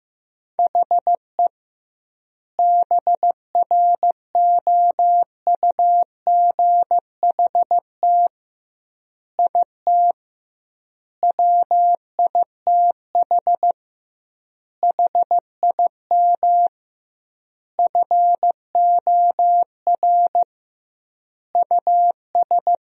Morse code, heard as a keyed tone, about 15 wpm, 700 Hz. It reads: HE BROUGHT IT WITH HIM FOR US
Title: 2x Word-Spacing - Sentences from Top 500 Words - 15wpm